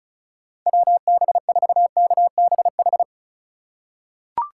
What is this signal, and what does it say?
Morse code, heard as a keyed tone, about 36 wpm, 700 Hz. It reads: WB4KBH
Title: USA Call Signs - 6 Characters - 3x Character Spacing - 35wpm